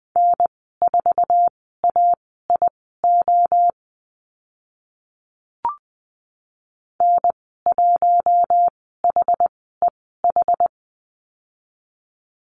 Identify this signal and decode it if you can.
Morse code, heard as a keyed tone, about 20 wpm, spaced as for 14 wpm, 700 Hz. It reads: N4AIO N1HEH
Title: 062 Lesson 11g - Letter h - Call Signs 20wpm